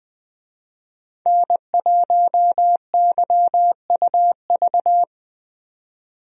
Morse code, heard as a keyed tone, about 20 wpm, 700 Hz. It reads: N1YUV
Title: USA Call Signs - 5 Characters - 20wpm